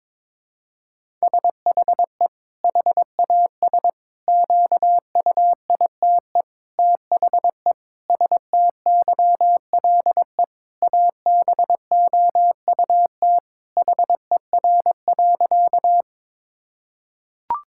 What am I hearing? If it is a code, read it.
Morse code, heard as a keyed tone, about 22 wpm, 700 Hz. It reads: SHE HAS QUITE THE STYLE ABOUT HER.
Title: Sentences from Top 1000 Words - Review Part 5 - 22wpm